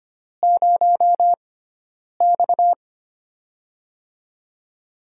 Morse code, heard as a keyed tone, about 25 wpm, 700 Hz. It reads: 0 X
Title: USA Call Signs - 6x Character Spacing - 25wpm